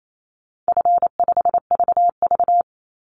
Morse code, heard as a keyed tone, about 28 wpm, 700 Hz. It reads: F5VV
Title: Call Signs - 28wpm